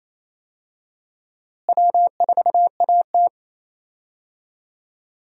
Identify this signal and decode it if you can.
Morse code, heard as a keyed tone, about 28 wpm, 700 Hz. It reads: W4AT
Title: USA Call Signs - 4 Characters - Repeated 2x - 28wpm